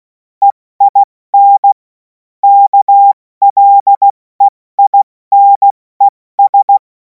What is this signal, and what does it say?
Morse code, heard as a keyed tone, about 16 wpm, 800 Hz. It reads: EIN KLEINES
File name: Morsekurs / Witze / Witz 03. März 2026 70 F80